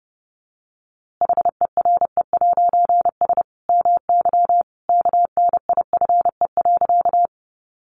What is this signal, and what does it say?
Morse code, heard as a keyed tone, about 30 wpm, 700 Hz. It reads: HERE'S MY KNIFE.
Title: A Sound Of Thunder 30wpm